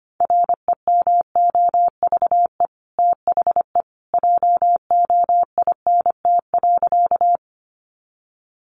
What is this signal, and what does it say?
Morse code, heard as a keyed tone, about 25 wpm, 700 Hz. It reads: REMOVE THE JOINT.
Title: Sentences from Top 1300 Words 25wpm